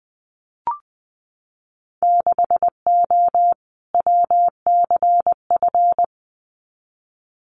Morse code, heard as a keyed tone, about 20 wpm, 700 Hz. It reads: 6O WCF
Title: Winter Field Day Exchanges Encore 20wpm